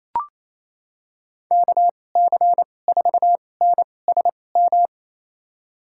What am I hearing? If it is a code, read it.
Morse code, heard as a keyed tone, about 28 wpm, spaced as for 20 wpm, 700 Hz. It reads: KC4NSM
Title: USA Call Signs - 6 Characters - 2x Character Spacing - 28wpm